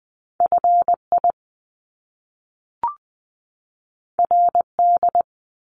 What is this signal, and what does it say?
Morse code, heard as a keyed tone, about 20 wpm, 700 Hz. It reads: FI RD
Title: Top Two Letter Combinations in Words 20wpm